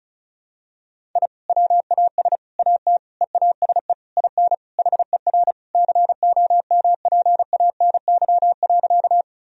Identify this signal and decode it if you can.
Morse code, heard as a keyed tone, about 36 wpm, 700 Hz. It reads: I WAS AT EASE IN HER COMPANY.
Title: Sentences from Top 1000 Words - Review Part 1 - 35wpm